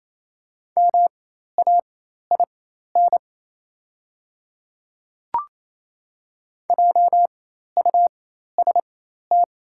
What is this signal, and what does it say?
Morse code, heard as a keyed tone, about 28 wpm, spaced as for 13 wpm, 700 Hz. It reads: MAIN JUST
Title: 4x Character Spacing - Top 500 Words - 28wpm